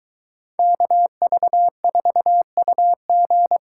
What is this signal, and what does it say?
Morse code, heard as a keyed tone, about 22 wpm, 700 Hz. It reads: KV4UG